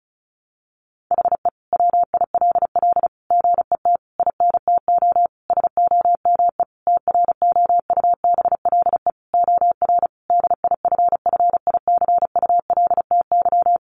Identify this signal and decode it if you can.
Morse code, heard as a keyed tone, about 34 wpm, 700 Hz. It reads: HE WILL GET INTO SOME TROUBLE OR DIFFICULTY
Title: Sentences from Top 1500 Words 35wpm